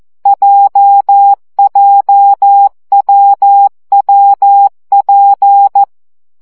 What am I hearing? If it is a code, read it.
Morse code, heard as a keyed tone, about 14 wpm, 800 Hz. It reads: JJWWP
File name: CW gyakorlas w p j 12 baud